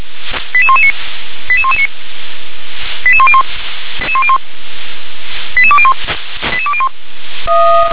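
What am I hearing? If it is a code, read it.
Five-tone selective call (ZVEI), heard as quick runs of five tones, tones of about 70 ms. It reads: 88119 88119 88101 88201 88201
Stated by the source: Five-tone ZVEI selective call received on SDR.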